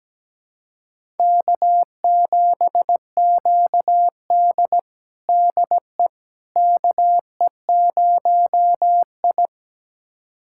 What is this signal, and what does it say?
Morse code, heard as a keyed tone, about 17 wpm, 700 Hz. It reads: K7QD DE KE0I
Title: USA Callsign DE Callsign - 4 Characters - 17wpm